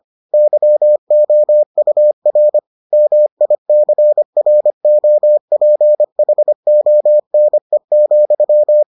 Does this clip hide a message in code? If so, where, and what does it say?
Yes: Morse code from 0.3 to 8.8 s, heard as a keyed tone, about 25 wpm, 600 Hz. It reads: YOUR MICROPHONE,